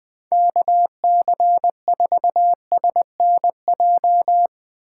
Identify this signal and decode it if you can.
Morse code, heard as a keyed tone, about 20 wpm, 700 Hz. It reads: KC4SNJ
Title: USA Call Signs - 6 Characters - 3x Character Spacing - 20wpm